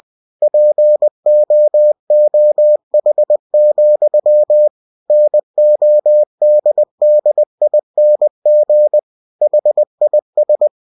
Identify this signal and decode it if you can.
Morse code, heard as a keyed tone, about 20 wpm, 600 Hz. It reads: POOH, NODDING HIS
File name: cw-books / winnie-the-pooh / winnie-20wpm-page-0030